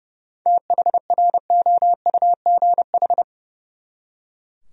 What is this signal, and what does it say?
Morse code, heard as a keyed tone, about 35 wpm, 700 Hz. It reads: THROUGH
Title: Speed-Racing Top 300 Words 30wpm